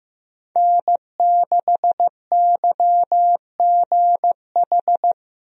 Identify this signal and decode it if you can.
Morse code, heard as a keyed tone, about 15 wpm, 700 Hz. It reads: N6YGH
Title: USA Call Signs - 5 Characters - Repeated 3x - 15wpm